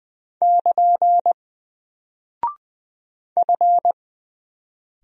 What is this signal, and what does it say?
Morse code, heard as a keyed tone, about 20 wpm, 700 Hz. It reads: <KN> F
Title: Single Character (Letter:Number:Punctuation:Prosign) 20wpm